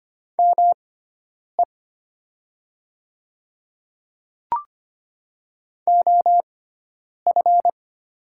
Morse code, heard as a keyed tone, about 25 wpm, spaced as for 9 wpm, 700 Hz. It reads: ME OF
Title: 6x Character Spacing - Top 300 Words - 25wpm